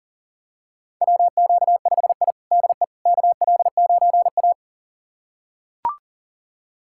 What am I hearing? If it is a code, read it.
Morse code, heard as a keyed tone, about 40 wpm, 700 Hz. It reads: WQ5I DE KL9A
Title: USA Callsign DE Callsign - 4 Characters - 40wpm